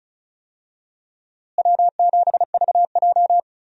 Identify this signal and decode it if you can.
Morse code, heard as a keyed tone, about 36 wpm, 700 Hz. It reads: W7VJ
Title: USA Call Signs - 4 Characters - Repeated 2x - 35wpm